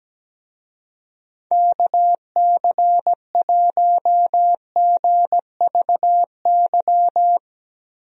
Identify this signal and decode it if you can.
Morse code, heard as a keyed tone, about 17 wpm, 700 Hz. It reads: KC1GVY
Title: USA Call Signs - 6 Characters - Repeated 2x - 17wpm